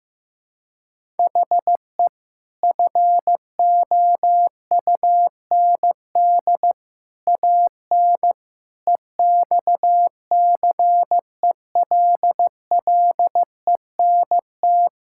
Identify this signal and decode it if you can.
Morse code, heard as a keyed tone, about 15 wpm, 700 Hz. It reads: HE FOUND AN EXCELLENT
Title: Sentences from Top 1000 Words - Review Part 6 - 15wpm